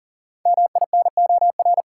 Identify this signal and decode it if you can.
Morse code, heard as a keyed tone, about 40 wpm, 700 Hz. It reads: MINOR
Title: Sentences from Top 1500 Words - New Word Drill Alt - 40wpm